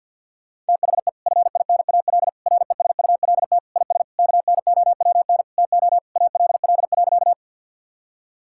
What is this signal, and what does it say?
Morse code, heard as a keyed tone, about 52 wpm, 700 Hz. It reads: THE FINAL RESULT IS KNOWN TO ALL.
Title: Sentences from Top 1000 Words - Review Part 5 - 50wpm